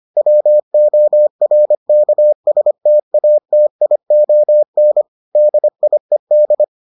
Morse code, heard as a keyed tone, about 25 wpm, 600 Hz. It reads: WORKSTATION DIED